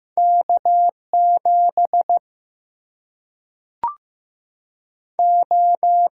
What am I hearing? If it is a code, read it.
Morse code, heard as a keyed tone, about 15 wpm, 700 Hz. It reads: K 7 O
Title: Single-Letter Single-Number 15wpm